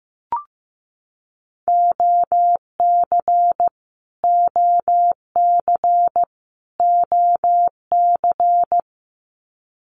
Morse code, heard as a keyed tone, about 15 wpm, 700 Hz. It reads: OC OC OC
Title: Abbreviations Repeated 3x - 15wpm